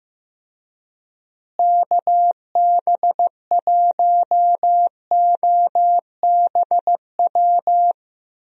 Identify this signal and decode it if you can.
Morse code, heard as a keyed tone, about 15 wpm, 700 Hz. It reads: KB1OBW
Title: USA Call Signs - 6 Characters - Repeated 3x - 15wpm